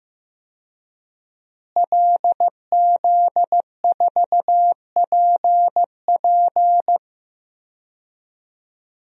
Morse code, heard as a keyed tone, about 15 wpm, 700 Hz. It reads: LZ4PP
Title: DXCC-100 Call Signs - 5 Characters - Repeated 3x - 15wpm